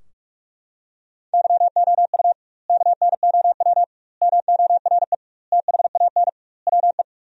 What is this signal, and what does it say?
Morse code, heard as a keyed tone, about 46 wpm, 700 Hz. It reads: YOU KNOW MORE THAN WE
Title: 2x Word-Spacing - Sentences from Top 100 Words - 45wpm